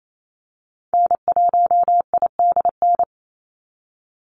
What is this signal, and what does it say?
Morse code, heard as a keyed tone, about 28 wpm, 700 Hz. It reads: N1IDN
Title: NA SOTA Activator Callsigns 28wpm